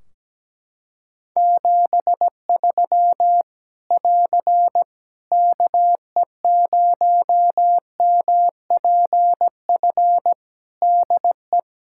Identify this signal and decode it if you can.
Morse code, heard as a keyed tone, about 17 wpm, 700 Hz. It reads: 73 <AR> KE0MPF DE
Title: Practice QSO - Exchange 4 - Ending - 17wpm